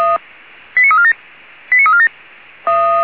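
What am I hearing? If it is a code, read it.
Five-tone selective call (ZVEI), heard as quick runs of five tones, tones of about 70 ms. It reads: 89347 89347
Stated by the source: ZVEI five-tone selective call received on an SDR.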